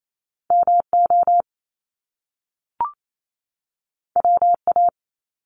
Morse code, heard as a keyed tone, about 28 wpm, 700 Hz. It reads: MO WA
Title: Top Two Letter Combinations in Words 28wpm